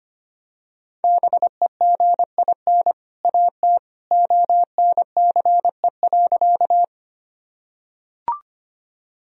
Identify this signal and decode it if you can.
Morse code, heard as a keyed tone, about 25 wpm, 700 Hz. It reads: BEGIN AT ONCE.
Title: Sentences from Top 1000 Words - Review Part 1 - 25wpm